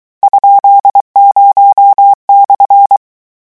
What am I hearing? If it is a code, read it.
Morse code, heard as a keyed tone, about 23 wpm, 800 Hz. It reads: ?0/
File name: morse mp3 / lecon13-vitesse 22